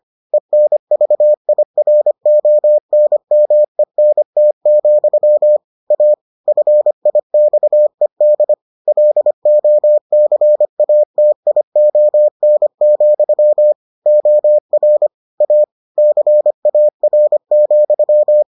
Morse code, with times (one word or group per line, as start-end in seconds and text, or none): 0.3-5.6 ENVIRONMENT,
5.9-6.1 A
6.5-8.5 FIXED
8.9-13.7 LOCATION,
14.1-15.1 OR
15.4-15.6 A
16.0-18.4 CAR,